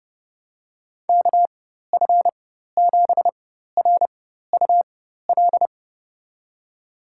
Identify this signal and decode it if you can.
Morse code, heard as a keyed tone, about 30 wpm, spaced as for 14 wpm, 700 Hz. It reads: KF7RUL
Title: USA Call Signs - 6 Characters - 4x Character Spacing - 30wpm